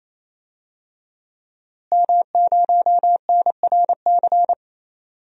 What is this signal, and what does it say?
Morse code, heard as a keyed tone, about 28 wpm, 700 Hz. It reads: M0NRC
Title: DXCC-100 Call Signs - 5 Characters - Repeated 3x - 28wpm